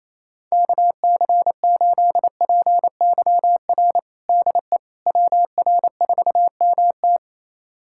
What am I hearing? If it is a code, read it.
Morse code, heard as a keyed tone, about 28 wpm, 700 Hz. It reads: KC8PYR DE WR4MT
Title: USA Callsign DE Callsign 28wpm